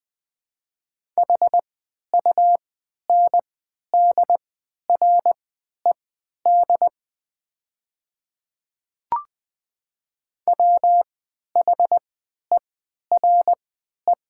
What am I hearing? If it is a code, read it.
Morse code, heard as a keyed tone, about 20 wpm, spaced as for 11 wpm, 700 Hz. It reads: HUNDRED WHERE
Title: 3x Character Spacing - Top 500 Words - 20wpm